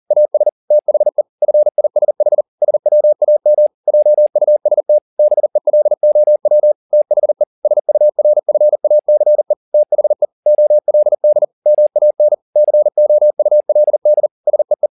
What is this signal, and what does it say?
Morse code, heard as a keyed tone, about 40 wpm, 600 Hz. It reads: AS THE FISH SWAM JUST BELOW THE SURFACE THE OLD MAN COULD SEE
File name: the old man and the sea 2 40wpm